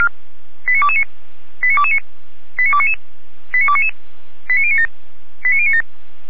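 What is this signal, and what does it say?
Five-tone selective call (ZVEI), heard as quick runs of five tones, tones of about 70 ms. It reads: 89229 89229 89299 89299 89097 89097
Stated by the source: ZVEI five-tone selective call received by SDR, from audio.f85360.feuerwehr-orlamuende.org (Feuerwehr Orlamünde fire brigade).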